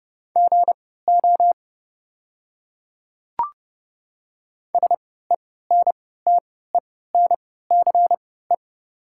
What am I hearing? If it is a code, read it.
Morse code, heard as a keyed tone, about 30 wpm, spaced as for 17 wpm, 700 Hz. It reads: GO SENTENCE
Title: 3x Character Spacing - Top 300 Words - 30wpm